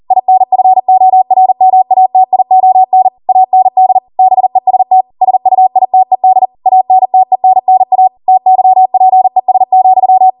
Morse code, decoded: INFORMATION AND BEST SUITED ANTENNA TYPES,

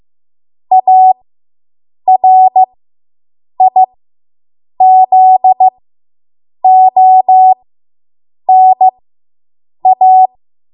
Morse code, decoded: ARIZONA

